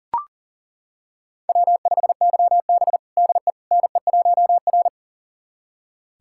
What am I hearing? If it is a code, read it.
Morse code, heard as a keyed tone, about 40 wpm, 700 Hz. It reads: W5YB DE NE1R
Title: USA Callsign DE Callsign - 4 Characters - 40wpm